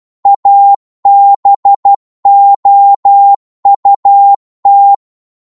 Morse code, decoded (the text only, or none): ABOUT